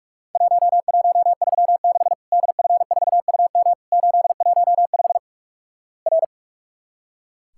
Morse code, heard as a keyed tone, about 46 wpm, 700 Hz. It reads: 1136 DF4UM 815 R
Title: Worked All Europe DX Contest - QTC Exchange - 45wpm